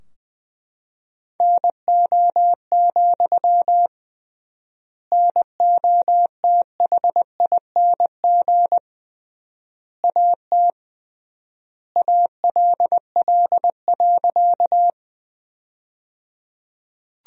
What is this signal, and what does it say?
Morse code, heard as a keyed tone, about 20 wpm, 700 Hz. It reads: NO, NOTHING AT ALL.
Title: 3x Word-Spacing - Sentences from Top 500 Words - 20wpm